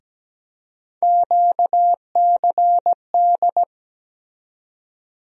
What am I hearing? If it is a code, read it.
Morse code, heard as a keyed tone, about 17 wpm, 700 Hz. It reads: QCD
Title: USA Call Sign Postfixes - 3 Letters 17wpm